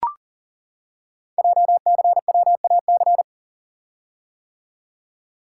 Morse code, heard as a keyed tone, about 40 wpm, 700 Hz. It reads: JCWAC
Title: Code Groups - 5 Letters - 40wpm